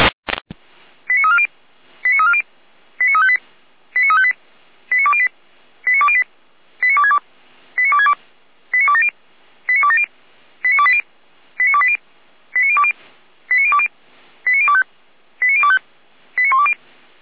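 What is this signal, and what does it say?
Five-tone selective call (ZVEI), heard as quick runs of five tones, tones of about 70 ms. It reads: 89340 89340 89358 89358 89208 89208 89262 89262 89270 89270 89280 89280 89020 89020 89025 89025 89120
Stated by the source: ZVEI five-tone call received by SDR at Feuerwehr Orlamünde fire brigade.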